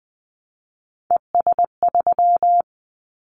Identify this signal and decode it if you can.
Morse code, heard as a keyed tone, about 20 wpm, 700 Hz. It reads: ES3
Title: Callsign Prefixes for Top 50 DXCC - 20wpm